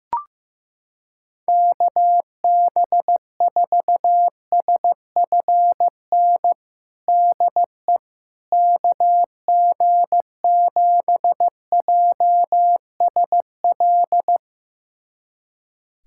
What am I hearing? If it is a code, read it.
Morse code, heard as a keyed tone, about 15 wpm, 700 Hz. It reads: KB4SFN DE KG7JSL
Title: USA Callsign DE Callsign - 6 Characters - 15wpm